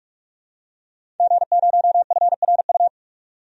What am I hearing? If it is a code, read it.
Morse code, heard as a keyed tone, about 46 wpm, 700 Hz. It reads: G0FRU
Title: DXCC-100 Call Signs - 5 Characters - 45wpm